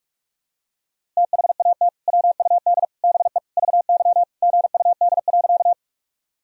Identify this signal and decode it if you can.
Morse code, heard as a keyed tone, about 47 wpm, 700 Hz. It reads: THAT WUD BE VY GUD.
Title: Sentences from Top 200 Words and Abbreviations 45wpm